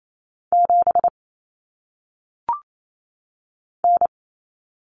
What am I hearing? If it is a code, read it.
Morse code, heard as a keyed tone, about 28 wpm, 700 Hz. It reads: 7 N
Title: Common Cut Numbers - 1-Digit 28wpm